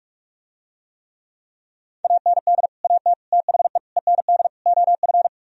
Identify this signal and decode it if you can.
Morse code, heard as a keyed tone, about 49 wpm, 700 Hz. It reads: AND AT THE END OF